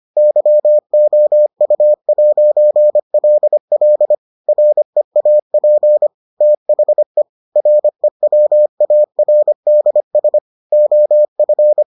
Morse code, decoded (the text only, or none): YOU'LL REAP THE REWARDS OF